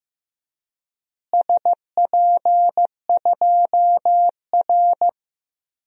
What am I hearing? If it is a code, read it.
Morse code, heard as a keyed tone, about 15 wpm, 700 Hz. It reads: SP2R
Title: DXCC-100 Call Signs - 4 Characters - Repeated 3x - 15wpm